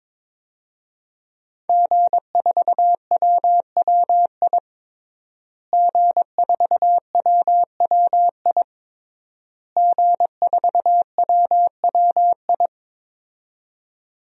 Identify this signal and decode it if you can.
Morse code, heard as a keyed tone, about 22 wpm, 700 Hz. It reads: G4WWI G4WWI G4WWI
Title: DXCC-100 Call Signs - 5 Characters - Repeated 3x - 22wpm